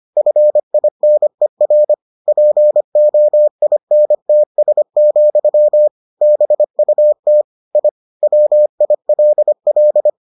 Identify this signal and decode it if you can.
Morse code, heard as a keyed tone, about 25 wpm, 600 Hz. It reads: FINER POINTS, BUT I WILL